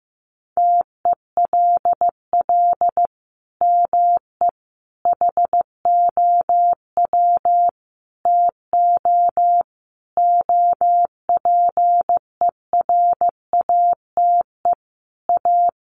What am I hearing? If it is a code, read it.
Morse code, heard as a keyed tone, about 15 wpm, 700 Hz. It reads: TELL ME HOW TO OPERATE A
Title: Sentences from Top 1500 Words 15wpm